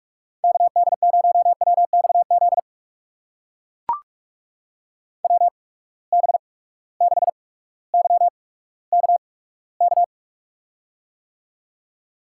Morse code, heard as a keyed tone, about 44 wpm, spaced as for 19 wpm, 700 Hz. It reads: KD0WXZ WB6YKK
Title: USA Call Signs - 6 Characters - 8x Character Spacing - 45wpm